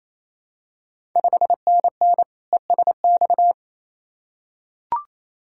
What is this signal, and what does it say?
Morse code, heard as a keyed tone, about 28 wpm, 700 Hz. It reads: 5NN ESX
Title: Ontario QSO Party Exchange 28wpm